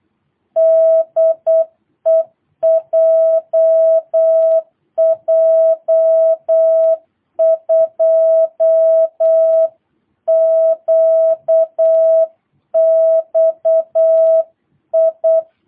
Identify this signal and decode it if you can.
Morse code, heard as a keyed tone, about 8 wpm, 660 Hz. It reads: DE JJ2QXI
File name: jh8cbh / 2022cw